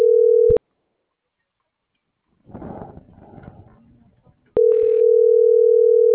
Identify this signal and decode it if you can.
Ringback tone (North America), 440 + 480 Hz.